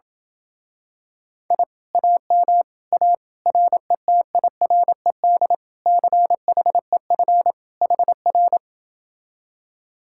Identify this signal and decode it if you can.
Morse code, heard as a keyed tone, about 27 wpm, 700 Hz. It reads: I AM A RETIRED CHEF HR